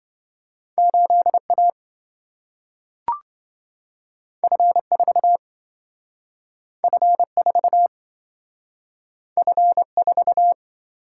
Morse code, heard as a keyed tone, about 27 wpm, 700 Hz. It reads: 8A F4 F4 F4